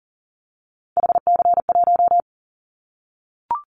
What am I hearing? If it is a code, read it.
Morse code, heard as a keyed tone, about 41 wpm, 700 Hz. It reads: HC1
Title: Callsign Prefixes for Top 100 DXCC - 40wpm